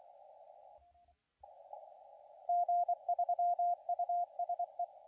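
Morse code, heard as a keyed tone, about 27 wpm, 700 Hz, with radio noise. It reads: G3USE
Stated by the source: CW contest audio recorded at LA4C during the SAC CW contest.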